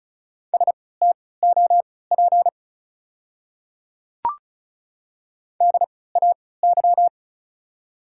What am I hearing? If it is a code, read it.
Morse code, heard as a keyed tone, about 34 wpm, spaced as for 20 wpm, 700 Hz. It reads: STOP DAY
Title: 3x Character Spacing - Top 300 Words - 35wpm